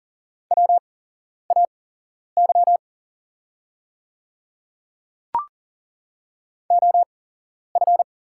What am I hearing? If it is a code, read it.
Morse code, heard as a keyed tone, about 39 wpm, spaced as for 11 wpm, 700 Hz. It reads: WAY OF
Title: 8x Character Spacing - Top 100 Words - 40wpm